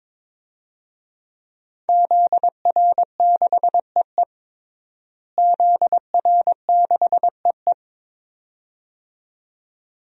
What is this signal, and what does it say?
Morse code, heard as a keyed tone, about 21 wpm, 700 Hz. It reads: ZR6EE ZR6EE
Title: DXCC-100 Call Signs - 5 Characters - Repeated 2x - 22wpm